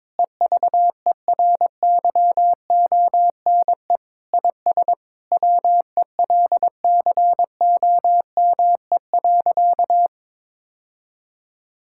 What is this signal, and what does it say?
Morse code, heard as a keyed tone, about 22 wpm, 700 Hz. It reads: EVERYONE IS WELCOME.
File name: Sentences with 7-Word Phrases - 22wpm